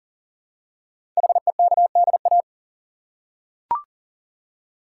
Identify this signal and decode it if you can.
Morse code, heard as a keyed tone, about 42 wpm, 700 Hz. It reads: HEKDA
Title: Code Groups - 5 Letters - 40wpm